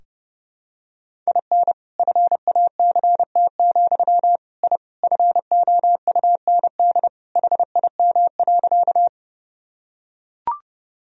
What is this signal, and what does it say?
Morse code, heard as a keyed tone, about 30 wpm, 700 Hz. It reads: IN FACT, I FOUND HIM.
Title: Sentences from Top 1000 Words - Review Part 3 - 30wpm